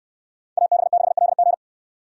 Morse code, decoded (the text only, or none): AB6LG